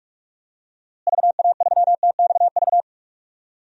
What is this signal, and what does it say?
Morse code, heard as a keyed tone, about 46 wpm, 700 Hz. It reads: VA3TXV
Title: DXCC-100 Call Signs - 6 Characters - Repeated 2x - 45wpm